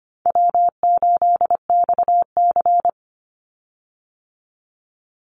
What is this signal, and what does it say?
Morse code, heard as a keyed tone, about 25 wpm, 700 Hz. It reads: W8XC
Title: Easy Call Signs - 25wpm